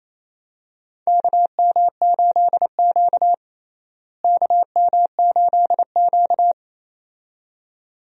Morse code, heard as a keyed tone, about 28 wpm, 700 Hz. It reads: KM8Q KM8Q
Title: USA Call Signs - 4 Characters - Repeated 2x - 28wpm